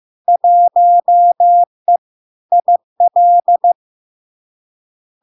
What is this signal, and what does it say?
Morse code, heard as a keyed tone, about 15 wpm, 700 Hz. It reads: 1E IL
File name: LARK-FD-2022-Log-in-CW-15wpm-700-Hz